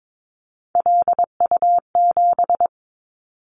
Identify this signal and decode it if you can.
Morse code, heard as a keyed tone, about 22 wpm, 700 Hz. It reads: LU7
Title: Callsign Prefixes for Top 50 DXCC - 22wpm